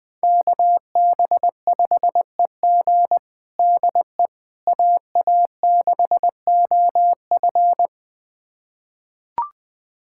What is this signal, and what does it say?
Morse code, heard as a keyed tone, about 21 wpm, 700 Hz. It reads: KB5EG DE AA6OF
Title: USA Callsign DE Callsign - 5 Characters - 20wpm